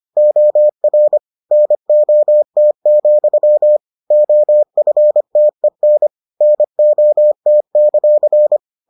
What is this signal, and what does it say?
Morse code, heard as a keyed tone, about 25 wpm, 600 Hz. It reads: OR NOT, OFTEN NOT;